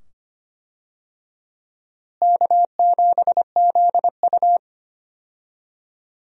Morse code, heard as a keyed tone, about 25 wpm, 700 Hz. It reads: K7ZU
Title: USA Call Signs - 4 Characters - Repeated 2x - 25wpm